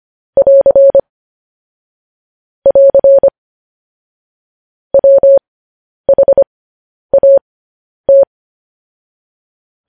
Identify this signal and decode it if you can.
Morse code, heard as a keyed tone, about 25 wpm, spaced as for 10 wpm, 550 Hz. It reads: <AR> <AR> WHAT
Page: news-headlines-in-morse-code-at-10-wpm